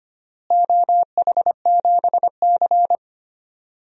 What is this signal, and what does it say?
Morse code, heard as a keyed tone, about 25 wpm, 700 Hz. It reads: OH7C
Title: DXCC-100 Call Signs - 4 Characters - Repeated 2x - 25wpm